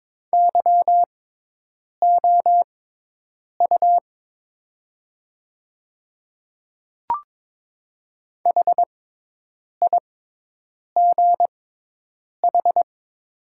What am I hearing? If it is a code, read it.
Morse code, heard as a keyed tone, about 21 wpm, spaced as for 8 wpm, 700 Hz. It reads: YOU HIGH
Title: 6x Character Spacing - Top 300 Words - 22wpm